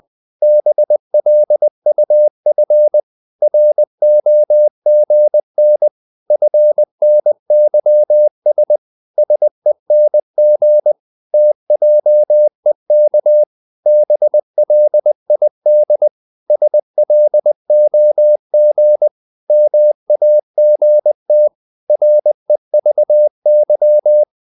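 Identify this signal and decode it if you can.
Morse code, heard as a keyed tone, about 20 wpm, 600 Hz. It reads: BLUF ROGN FNYS SENG TJEK BLID SLOG MAGT REVY
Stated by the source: Morse code practice of short Danish words at 20 wpm, from CW med Gnister.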